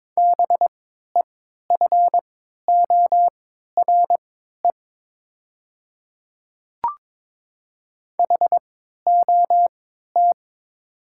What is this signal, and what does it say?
Morse code, heard as a keyed tone, about 22 wpm, spaced as for 12 wpm, 700 Hz. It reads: BEFORE HOT
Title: 3x Character Spacing - Top 200 Words - 22wpm